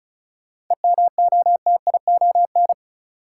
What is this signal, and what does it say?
Morse code, heard as a keyed tone, about 36 wpm, 700 Hz. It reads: EMOTION
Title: Sentences from Top 1600 Words - New Word Drill - 35wpm